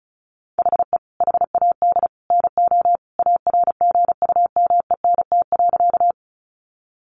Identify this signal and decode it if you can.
Morse code, heard as a keyed tone, about 34 wpm, 700 Hz. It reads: HE HAD NO ARGUMENT.
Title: Sentences from Top 1400 Words 35wpm